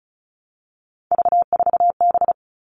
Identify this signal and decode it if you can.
Morse code, heard as a keyed tone, about 34 wpm, 700 Hz. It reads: V4B
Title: All Cut Numbers - 3-Digit 35wpm